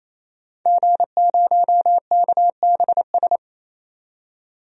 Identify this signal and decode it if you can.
Morse code, heard as a keyed tone, about 28 wpm, 700 Hz. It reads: G0KBS
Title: DXCC-100 Call Signs - 5 Characters - 28wpm